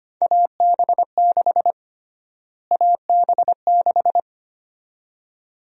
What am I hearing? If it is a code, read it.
Morse code, heard as a keyed tone, about 25 wpm, 700 Hz. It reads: AB6 AB6
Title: USA Call Sign Prefixes - Repeated 2x - 25wpm